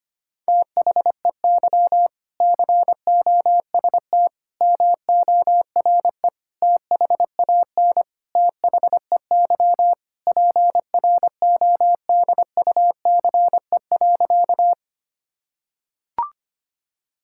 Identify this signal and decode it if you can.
Morse code, heard as a keyed tone, about 25 wpm, 700 Hz. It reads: THEY COST MORE THAN THEY PRODUCE.
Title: Sentences from Top 1000 Words - Review Part 5 - 25wpm